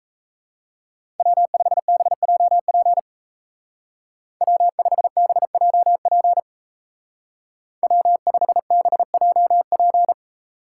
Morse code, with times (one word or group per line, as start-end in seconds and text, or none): 1.2-3.0 W5BJP
4.4-6.4 W5BJP
7.8-10.1 W5BJP